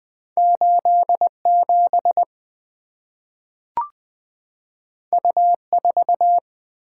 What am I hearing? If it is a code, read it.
Morse code, heard as a keyed tone, about 20 wpm, 700 Hz. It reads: 87 U4
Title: Two Letter-Number Combinations - 20wpm